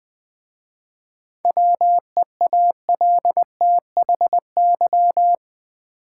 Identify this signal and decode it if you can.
Morse code, heard as a keyed tone, about 20 wpm, 690 Hz. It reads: WEALTHY
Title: 7 Letter Words - First 3 Letters Given - 20wpm